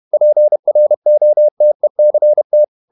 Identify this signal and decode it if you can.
Morse code, heard as a keyed tone, about 31 wpm, 600 Hz. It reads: PROTECT
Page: Never 31wpm | CW med Gnister